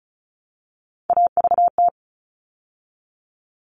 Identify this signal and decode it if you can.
Morse code, heard as a keyed tone, about 34 wpm, 700 Hz. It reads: AVT
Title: All Cut Numbers - 3-Digit 35wpm